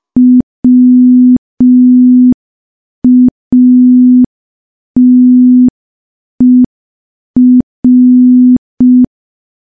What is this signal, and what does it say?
Morse code, heard as a keyed tone, about 5 wpm, 250 Hz. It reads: WATER
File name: morse / morse-training-05WPM